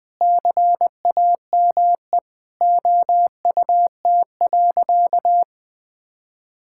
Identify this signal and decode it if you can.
Morse code, heard as a keyed tone, about 20 wpm, 700 Hz. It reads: CAME OUT.
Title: Sentences with 9-Word Phrases - 20wpm